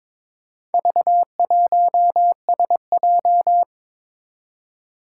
Morse code, heard as a keyed tone, about 22 wpm, 700 Hz. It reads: V1SJ
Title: Four Challenging Letter-Number Combinations 22wpm